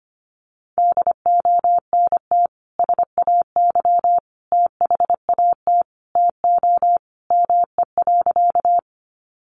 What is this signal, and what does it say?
Morse code, heard as a keyed tone, about 25 wpm, 700 Hz. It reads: DONT SAY THAT TO ME.
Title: Sentences from Top 300 Words 25wpm